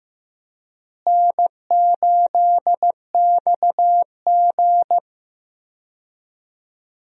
Morse code, heard as a keyed tone, about 15 wpm, 700 Hz. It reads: N8XG
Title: USA Call Signs - 4 Characters - Repeated 3x - 15wpm